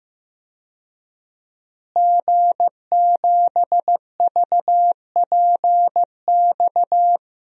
Morse code, heard as a keyed tone, about 15 wpm, 700 Hz. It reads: G7VPX
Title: DXCC-100 Call Signs - 5 Characters - Repeated 3x - 15wpm